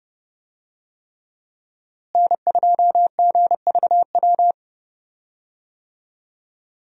Morse code, heard as a keyed tone, about 31 wpm, 700 Hz. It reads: N2GVW
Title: USA Call Signs - 5 Characters - Repeated 3x - 30wpm